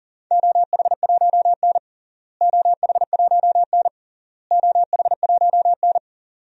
Morse code, heard as a keyed tone, about 40 wpm, 700 Hz. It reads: OH1N OH1N OH1N
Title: DXCC-100 Call Signs - 4 Characters - Repeated 3x - 40wpm